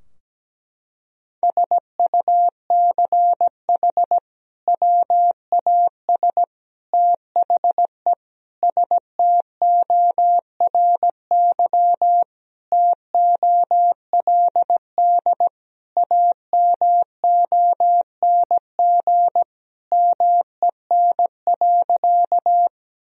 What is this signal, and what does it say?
Morse code, heard as a keyed tone, about 17 wpm, 700 Hz. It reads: SUCH WAS THE STORY TOLD AMONG MEN.
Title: Sentences from Top 1000 Words - Review Part 3 - 17wpm